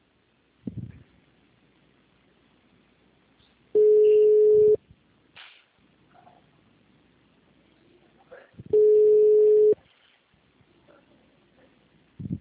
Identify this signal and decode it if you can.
Ringback tone (Europe, 425 Hz), with speech.